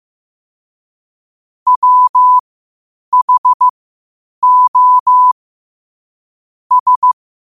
Morse code, heard as a keyed tone, about 15 wpm, spaced as for 8 wpm, 1000 Hz. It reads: WHO S
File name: qotdfiles / 8wpm